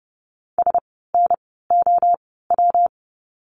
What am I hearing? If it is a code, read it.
Morse code, heard as a keyed tone, about 30 wpm, 700 Hz. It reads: S N O W
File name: course / 30:10wpm / 115 Lesson 19f - Letter w - Words 30wpm